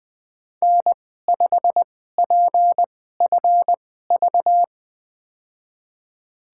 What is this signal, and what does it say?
Morse code, heard as a keyed tone, about 20 wpm, spaced as for 14 wpm, 700 Hz. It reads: N5PFV
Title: USA Call Signs - 2x Character Spacing - 20wpm